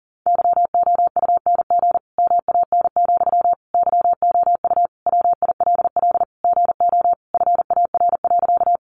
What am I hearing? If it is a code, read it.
Morse code, heard as a keyed tone, about 40 wpm, 700 Hz. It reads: YOUNG MAN, YOU WILL GO FAR.
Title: Sentences from Top 400 Words 40wpm